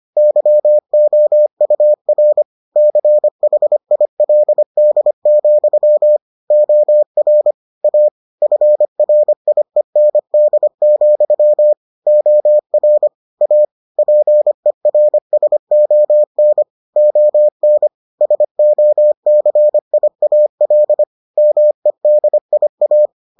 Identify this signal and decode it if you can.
Morse code, heard as a keyed tone, about 25 wpm, 600 Hz. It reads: YOUR CHILD, OR A FRIEND, OR A PERSON ON SOCIAL MEDIA